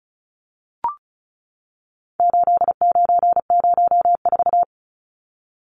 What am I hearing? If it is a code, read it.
Morse code, heard as a keyed tone, about 34 wpm, 700 Hz. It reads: 8904
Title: Numbers - 4-Digit 35wpm